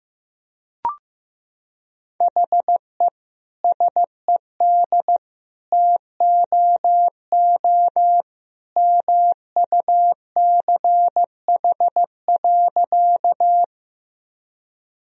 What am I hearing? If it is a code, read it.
Morse code, heard as a keyed tone, about 15 wpm, 700 Hz. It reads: HE SED TOO MUCH.
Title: Sentences from Top 200 Words and Abbreviations 15wpm